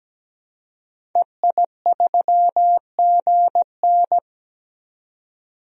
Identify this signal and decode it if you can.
Morse code, heard as a keyed tone, about 17 wpm, 700 Hz. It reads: EI3GN
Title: DXCC-100 Call Signs - 5 Characters - 17wpm